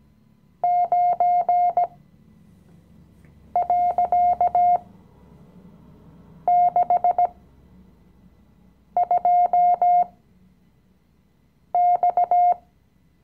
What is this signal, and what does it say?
Morse code, heard as a keyed tone, about 17 wpm, spaced as for 5 wpm, 710 Hz. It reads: 9.62X
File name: CW-Online-Kurs / Anfänger / Lektion 017